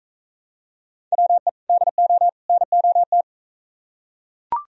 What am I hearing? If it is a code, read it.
Morse code, heard as a keyed tone, about 42 wpm, 700 Hz. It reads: WE DO NOT